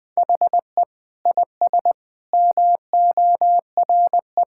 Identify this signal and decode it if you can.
Morse code, heard as a keyed tone, about 20 wpm, 700 Hz. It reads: HE IS MORE
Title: Sentences with 4-Word Phrases - 20wpm